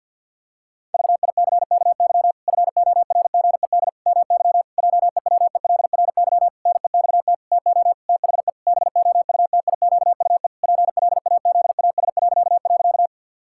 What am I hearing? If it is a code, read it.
Morse code, heard as a keyed tone, about 50 wpm, 700 Hz. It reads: VICKY FORGED MY JEWELRY NEXT TO THE BOUTIQUE PLAZAS..
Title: Pangram Challenge v2 - 50wpm